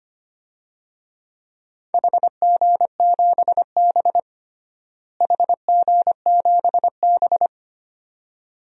Morse code, heard as a keyed tone, about 25 wpm, 700 Hz. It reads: HG7B HG7B
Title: DXCC-100 Call Signs - 4 Characters - Repeated 2x - 25wpm